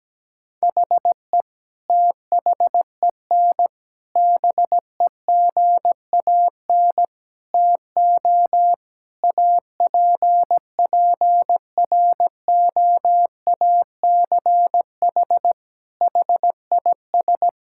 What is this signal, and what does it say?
Morse code, heard as a keyed tone, about 17 wpm, 700 Hz. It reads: HE THEN BEGAN TO APPROACH HIS
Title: Sentences from Top 1000 Words - Review Part 6 - 17wpm